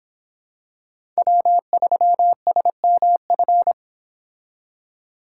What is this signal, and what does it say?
Morse code, heard as a keyed tone, about 25 wpm, 700 Hz. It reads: W3SMF